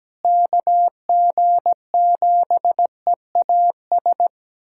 Morse code, heard as a keyed tone, about 17 wpm, 700 Hz. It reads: KG7EAS
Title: USA Callsign DE Callsign - 6 Characters - 17wpm